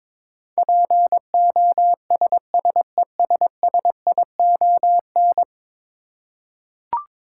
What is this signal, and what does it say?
Morse code, heard as a keyed tone, about 22 wpm, 700 Hz. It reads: POSSESSION
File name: practice / 22wpm / 10 Letter Words - First 3 Letters Given - 22wpm